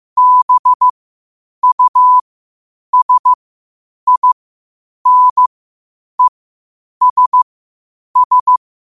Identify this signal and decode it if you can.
Morse code, heard as a keyed tone, about 15 wpm, spaced as for 8 wpm, 1000 Hz. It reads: BUSINESS
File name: qotdfiles / 8wpm